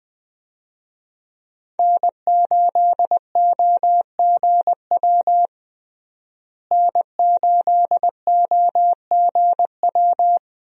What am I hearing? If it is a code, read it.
Morse code, heard as a keyed tone, about 20 wpm, 700 Hz. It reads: N8OGW N8OGW
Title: USA Call Signs - 5 Characters - Repeated 2x - 20wpm